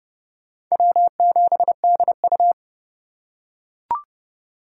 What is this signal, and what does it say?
Morse code, heard as a keyed tone, about 29 wpm, 700 Hz. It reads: W7DU